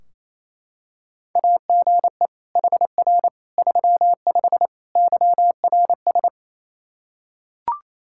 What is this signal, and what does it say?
Morse code, heard as a keyed tone, about 28 wpm, 700 Hz. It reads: AGE HR 35 YRS
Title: K1USN SST QSO Practice 28wpm